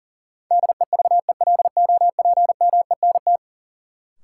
Morse code, heard as a keyed tone, about 40 wpm, 700 Hz. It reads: DEVELOPMENT
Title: Sentences from Top 1400 Words - New Word Drill Alt - 40wpm